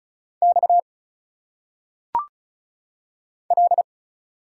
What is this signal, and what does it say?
Morse code, heard as a keyed tone, about 34 wpm, 700 Hz. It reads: X L
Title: Single Letters - Phonetic Pronunciation 35wpm